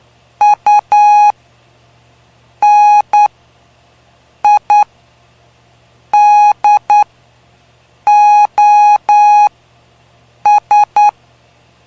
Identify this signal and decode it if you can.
Morse code, heard as a keyed tone, about 9 wpm, spaced as for 5 wpm, 830 Hz. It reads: UNIDOS
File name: cw / py2lcd / 05-PY2LCD-Li��o 5